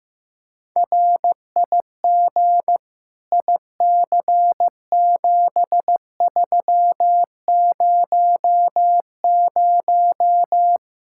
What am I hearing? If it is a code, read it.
Morse code, heard as a keyed tone, about 15 wpm, 700 Hz. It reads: RIG IC7300
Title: Practice QSO - Exchange 2 - Info - 15wpm